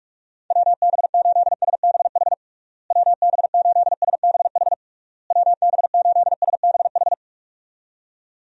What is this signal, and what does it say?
Morse code, heard as a keyed tone, about 46 wpm, 700 Hz. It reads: WB8SBH WB8SBH WB8SBH
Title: USA Call Signs - 6 Characters - Repeated 3x - 45wpm